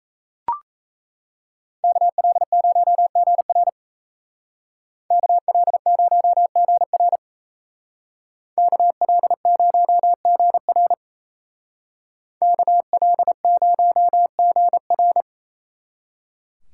Morse code, heard as a keyed tone, about 34 wpm, 700 Hz. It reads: KL0GR KL0GR KL0GR KL0GR
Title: Speed Racing - USA 2x2 Call Signs 28wpm